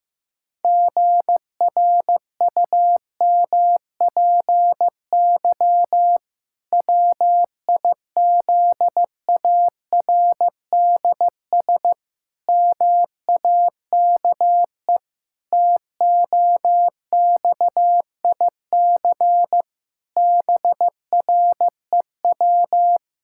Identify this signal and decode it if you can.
Morse code, heard as a keyed tone, about 15 wpm, 700 Hz. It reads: GRUMPY WIZARDS MAKE TOXIC BREW
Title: Pangram Challenge v2 - 15wpm